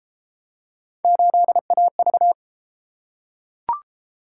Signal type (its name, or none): Morse code